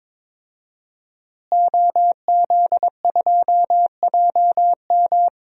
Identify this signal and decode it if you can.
Morse code, heard as a keyed tone, about 22 wpm, 700 Hz. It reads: OZ2JM
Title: DXCC-100 Call Signs - 5 Characters - Repeated 3x - 22wpm